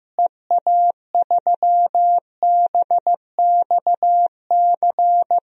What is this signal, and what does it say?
Morse code, heard as a keyed tone, about 15 wpm, 700 Hz. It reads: EA3BXC
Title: DXCC-100 Call Signs - 6 Characters - Repeated 3x - 15wpm